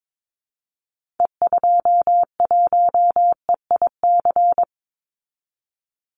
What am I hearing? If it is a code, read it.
Morse code, heard as a keyed tone, about 22 wpm, 700 Hz. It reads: E21EIC
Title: Call Signs - 22wpm